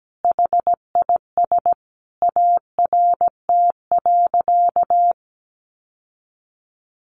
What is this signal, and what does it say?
Morse code, heard as a keyed tone, about 17 wpm, 700 Hz. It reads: HIS ART.
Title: Sentences from Top 1200 Words 17wpm